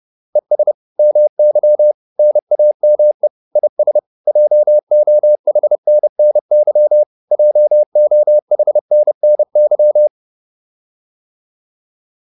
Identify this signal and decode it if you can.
Morse code, heard as a keyed tone, about 30 wpm, 600 Hz. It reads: ES MY NAME IS JOHNNY JOHNNY